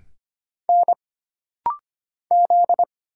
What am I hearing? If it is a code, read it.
Morse code, heard as a keyed tone, about 25 wpm, 700 Hz. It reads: N Z
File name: Single Character (Letter:Number:Punctuation:Prosign) - Rapid-Fire - 25wpm